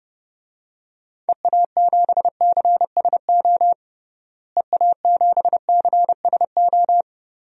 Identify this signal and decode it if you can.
Morse code, heard as a keyed tone, about 31 wpm, 700 Hz. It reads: EA7CSO EA7CSO
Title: DXCC-100 Call Signs - 6 Characters - Repeated 3x - 30wpm